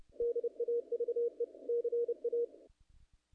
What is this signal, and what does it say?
Morse code, heard as a keyed tone, about 28 wpm, 440 Hz. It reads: DAVE CA
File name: contests / naqp-cw-k5tr-op-k5pi- -1800